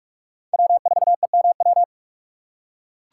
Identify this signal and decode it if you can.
Morse code, heard as a keyed tone, about 43 wpm, 700 Hz. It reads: W4EMW